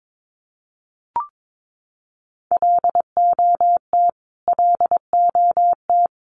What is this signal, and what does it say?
Morse code, heard as a keyed tone, about 22 wpm, 700 Hz. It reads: LOT LOT
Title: Top 500 Words Repeated 22wpm